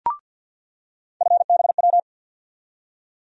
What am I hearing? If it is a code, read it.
Morse code, heard as a keyed tone, about 48 wpm, 700 Hz. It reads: FBW
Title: USA Call Sign Postfixes - 3 Letters 50wpm